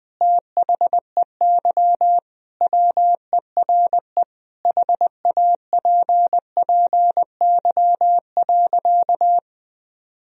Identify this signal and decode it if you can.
Morse code, heard as a keyed tone, about 20 wpm, 700 Hz. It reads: THEY WERE HAPPY.
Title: Sentences from Top 1000 Words - Review Part 1 - 20wpm